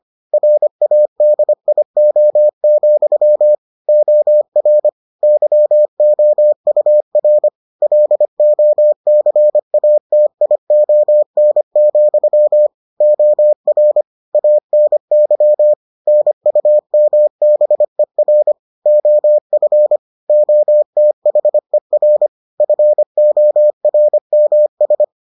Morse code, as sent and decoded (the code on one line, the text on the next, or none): .-. .- -.. .. --- --..-- / --- .-. / -.-- --- ..- .-. / .-.. --- -.-. .- - .. --- -. --..-- / --- .-. / .- -. -.-- / -. ..- -- -... .-. / --- ..-. / --- - .... .-. / ..-. --- .-. -- ...
RADIO, OR YOUR LOCATION, OR ANY NUMBER OF OTHER FORMS